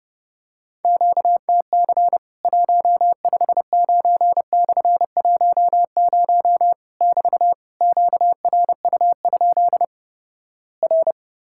Morse code, heard as a keyed tone, about 31 wpm, 700 Hz. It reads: QTC 159/10 <BT> QRU? R
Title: Worked All Europe DX Contest - QTC Exchange - 30wpm